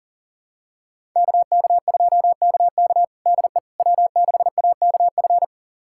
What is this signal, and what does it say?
Morse code, heard as a keyed tone, about 40 wpm, 700 Hz. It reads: KK2KK DE W6AKF
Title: USA Callsign DE Callsign - 5 Characters - 40wpm